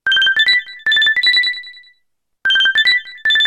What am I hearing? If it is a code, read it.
There is no code.